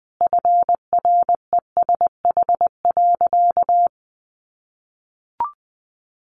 Morse code, heard as a keyed tone, about 20 wpm, 700 Hz. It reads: FRESH.
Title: Sentences from Top 1200 Words 20wpm